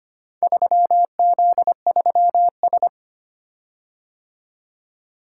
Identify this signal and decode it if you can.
Morse code, heard as a keyed tone, about 25 wpm, 700 Hz. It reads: 3Z3S
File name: practice / 25wpm / Four Letter-Number Combinations - 25wpm